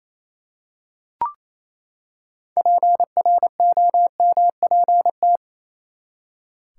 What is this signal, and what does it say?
Morse code, heard as a keyed tone, about 27 wpm, 700 Hz. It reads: PROMPT
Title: Six Letter Words - 28wpm